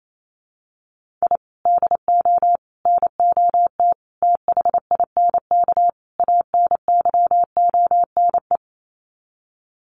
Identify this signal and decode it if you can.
Morse code, heard as a keyed tone, about 28 wpm, 700 Hz. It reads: I DO NOT THINK ANYONE
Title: Sets of 5 Words Encore 28wpm